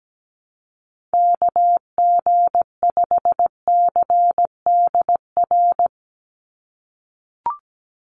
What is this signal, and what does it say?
Morse code, heard as a keyed tone, about 17 wpm, 700 Hz. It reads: KG5CDR
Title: USA Call Signs 17wpm